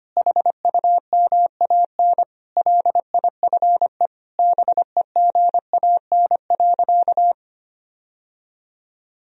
Morse code, heard as a keyed tone, about 25 wpm, 700 Hz. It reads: HUMAN LIFE BEGAN.
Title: Sentences from Top 1000 Words - Review Part 3 - 25wpm